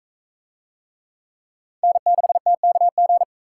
Morse code, heard as a keyed tone, about 44 wpm, 700 Hz. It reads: N6TKG